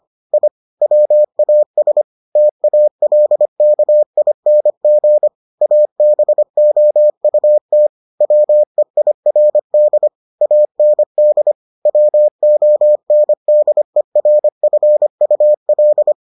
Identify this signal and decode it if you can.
Morse code, heard as a keyed tone, about 25 wpm, 600 Hz. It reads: I WAS TALKING ABOUT WEIRD AND WONDERFUL